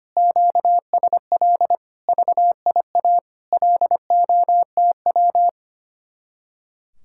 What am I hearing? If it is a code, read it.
Morse code, heard as a keyed tone, about 32 wpm, 700 Hz. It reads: QSL VIA LOTW